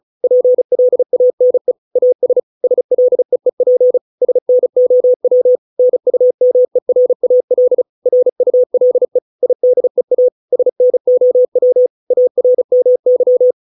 Morse code, heard as a keyed tone, about 36 wpm, 500 Hz. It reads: PLANE AS SLEEP SNOW NUMERAL RULE IDEA SNOW ARMY